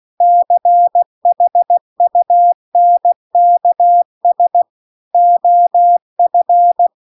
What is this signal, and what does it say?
Morse code, heard as a keyed tone, about 16 wpm, 700 Hz. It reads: CHUNKS OF